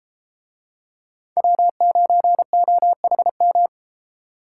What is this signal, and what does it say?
Morse code, heard as a keyed tone, about 32 wpm, 700 Hz. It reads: W9OHM